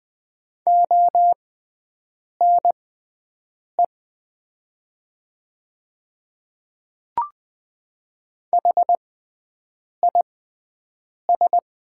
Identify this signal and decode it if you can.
Morse code, heard as a keyed tone, about 20 wpm, spaced as for 7 wpm, 700 Hz. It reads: ONE HIS
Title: 6x Character Spacing - Top 100 Words - 20wpm